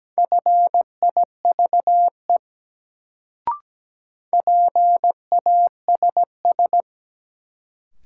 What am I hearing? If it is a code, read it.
Morse code, heard as a keyed tone, about 17 wpm, 700 Hz. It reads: FIVE PASS
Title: Sentences from Top 400 Words - New Word Drill - 17wpm